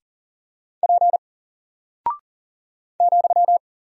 Morse code, heard as a keyed tone, about 40 wpm, 700 Hz. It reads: P ,
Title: Single Character (Letter:Number:Punctuation:Prosign) Alt 40wpm